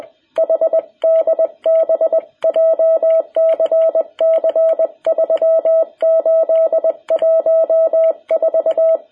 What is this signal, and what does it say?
Morse code, heard as a keyed tone, about 20 wpm, 660 Hz. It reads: HDB JCC 3814
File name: jh8cbh / 2021cw